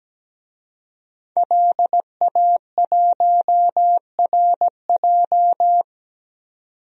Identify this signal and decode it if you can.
Morse code, heard as a keyed tone, about 17 wpm, 700 Hz. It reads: LA1RJ
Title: DXCC-100 Call Signs - 5 Characters - 17wpm